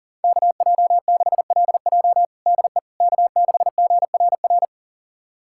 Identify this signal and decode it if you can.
Morse code, heard as a keyed tone, about 37 wpm, 700 Hz. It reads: KJ6LJ DE K6GRR
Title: USA Callsign DE Callsign - 5 Characters - 40wpm